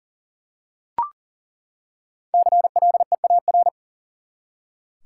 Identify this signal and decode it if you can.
Morse code, heard as a keyed tone, about 40 wpm, 700 Hz. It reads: CLEAR
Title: Five Letter Words - 40wpm